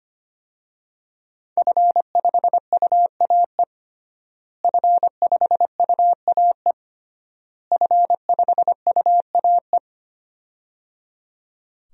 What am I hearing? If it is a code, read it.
Morse code, heard as a keyed tone, about 25 wpm, 700 Hz. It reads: F5UAE F5UAE F5UAE
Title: DXCC-100 Call Signs - 5 Characters - Repeated 3x - 25wpm